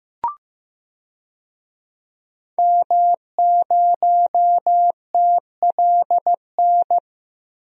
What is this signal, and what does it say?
Morse code, heard as a keyed tone, about 16 wpm, 700 Hz. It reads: M0TLN
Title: DXCC-100 Call Signs - 5 Characters - 15wpm